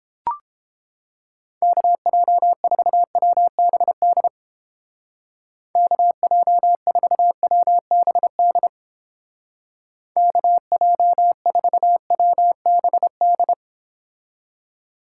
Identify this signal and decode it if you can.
Morse code, heard as a keyed tone, about 32 wpm, 700 Hz. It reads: KJ4WBD KJ4WBD KJ4WBD